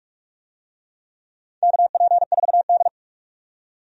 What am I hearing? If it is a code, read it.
Morse code, heard as a keyed tone, about 44 wpm, 700 Hz. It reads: KP4D
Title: DXCC-100 Call Signs - 4 Characters - 45wpm